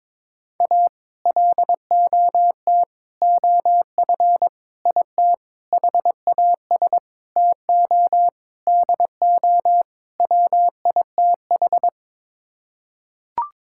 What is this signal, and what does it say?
Morse code, heard as a keyed tone, about 22 wpm, 700 Hz. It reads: A LOT OF IT HAS TO DO WITH
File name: practice / 22wpm / Sets of 8 Words - 22wpm